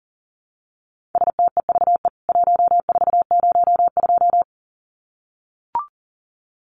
Morse code, heard as a keyed tone, about 40 wpm, 700 Hz. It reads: STEVE 1402
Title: CWT Exchange 40wpm